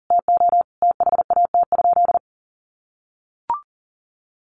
Morse code, heard as a keyed tone, about 40 wpm, 700 Hz. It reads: TO THAT?
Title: Sentences from Top 1100 Words 40wpm